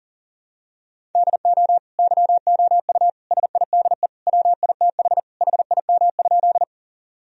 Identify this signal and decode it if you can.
Morse code, heard as a keyed tone, about 37 wpm, 700 Hz. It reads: DO YOU SIDE WITH HIM?
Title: Sentences from Top 1000 Words - Review Part 1 - 40wpm